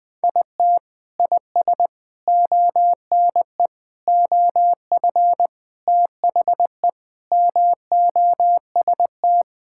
Morse code, heard as a keyed tone, about 20 wpm, 700 Hz. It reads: IT IS ONE OF THE MOST
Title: Sentences with 4-Word Phrases - Spoken Beginning - 20wpm